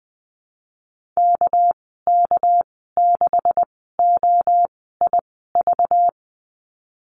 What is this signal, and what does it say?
Morse code, heard as a keyed tone, about 20 wpm, spaced as for 15 wpm, 700 Hz. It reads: KK6OIV
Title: 219 Lesson 35g - Number 0 - Call Signs 20wpm